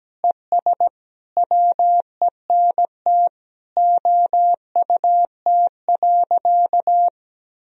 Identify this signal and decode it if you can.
Morse code, heard as a keyed tone, about 17 wpm, 700 Hz. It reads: ES WENT OUT.
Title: Sentences from Top 300 Words and Abbreviations 17wpm